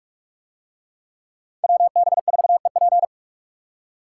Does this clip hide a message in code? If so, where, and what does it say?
Yes: Morse code from 1.6 to 3.0 s, heard as a keyed tone, about 44 wpm, 700 Hz. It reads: WB4EP